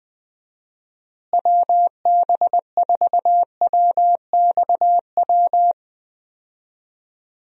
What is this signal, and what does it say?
Morse code, heard as a keyed tone, about 21 wpm, 700 Hz. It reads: WB4WXW
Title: USA Call Signs - 6 Characters - Repeated 2x - 20wpm